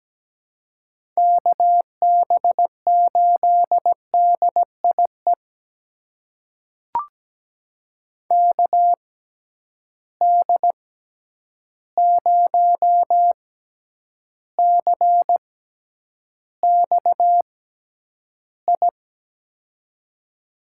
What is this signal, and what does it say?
Morse code, heard as a keyed tone, about 17 wpm, spaced as for 9 wpm, 700 Hz. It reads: KB8DIE KD0CXI
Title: USA Call Signs - 6 Characters - 6x Character Spacing - 17wpm